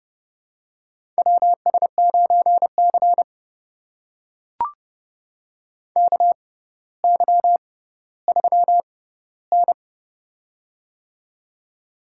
Morse code, heard as a keyed tone, about 31 wpm, spaced as for 16 wpm, 700 Hz. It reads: WS9C KY3N
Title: USA Call Signs - 4 Characters - 6x Character Spacing - 30wpm